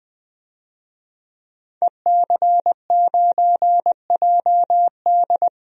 Morse code, heard as a keyed tone, about 20 wpm, 700 Hz. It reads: EC9JD
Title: DXCC-100 Call Signs - 5 Characters - 20wpm